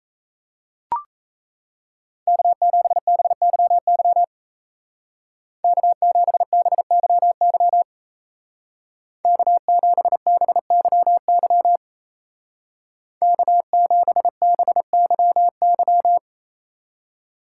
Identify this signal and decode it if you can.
Morse code, heard as a keyed tone, about 34 wpm, 700 Hz. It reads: K7BYY K7BYY K7BYY K7BYY
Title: Speed Racing - USA 1x3 Call Signs 28wpm